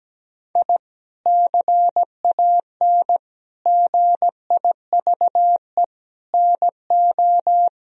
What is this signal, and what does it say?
Morse code, heard as a keyed tone, about 17 wpm, 700 Hz. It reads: I CAN GIVE NO
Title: Sentences from Top 1000 Words - Review Part 3 - 17wpm